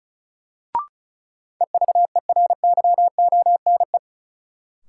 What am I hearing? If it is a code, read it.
Morse code, heard as a keyed tone, about 36 wpm, 700 Hz. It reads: EVERYONE
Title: Sentences from Top 800 Words - New Word Drill - 35wpm